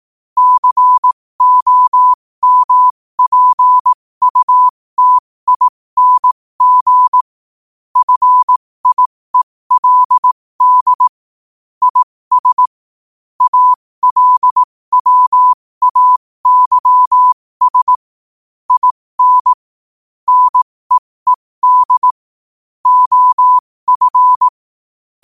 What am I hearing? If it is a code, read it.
Morse code, heard as a keyed tone, about 18 wpm, spaced as for 15 wpm, 1000 Hz. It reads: COMPUTING FIELD IS ALWAYS IN NEED OF
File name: qotdfiles / 15wpm